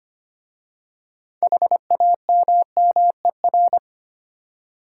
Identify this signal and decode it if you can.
Morse code, heard as a keyed tone, about 25 wpm, 700 Hz. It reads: HAMMER
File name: practice / 25wpm / Easy Words with Hints - 25wpm